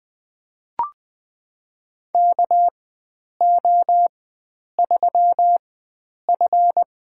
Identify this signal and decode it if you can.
Morse code, heard as a keyed tone, about 20 wpm, spaced as for 9 wpm, 700 Hz. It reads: KO3F
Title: USA Call Signs - 4 Characters - 4x Character Spacing - 20wpm